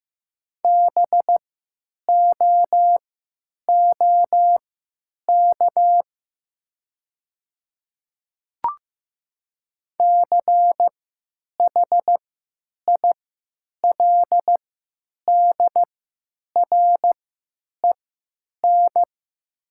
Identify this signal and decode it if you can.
Morse code, heard as a keyed tone, about 15 wpm, spaced as for 9 wpm, 700 Hz. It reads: BOOK CHILDREN
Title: 3x Character Spacing - Top 300 Words - 15wpm